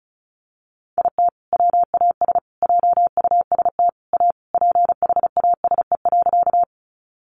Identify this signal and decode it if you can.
Morse code, heard as a keyed tone, about 34 wpm, 700 Hz. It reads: IT WAS JUST A PHASE.
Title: Sentences from Top 1200 Words 35wpm